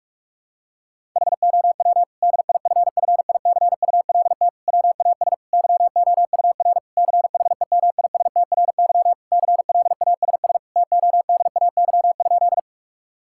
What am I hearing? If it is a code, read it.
Morse code, heard as a keyed tone, about 46 wpm, 700 Hz. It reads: HOW DIFFICULT WAS YOUR CHEMISTRY CLASS TODAY?
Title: Sentences from Top 1000 Words - Review Part 6 - 45wpm